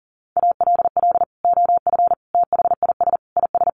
Morse code, heard as a keyed tone, about 40 wpm, 700 Hz. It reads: ALL OF THIS IS
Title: Sentences from Top 1100 Words 40wpm